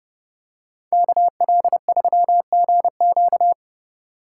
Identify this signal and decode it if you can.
Morse code, heard as a keyed tone, about 29 wpm, 700 Hz. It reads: KL3GQ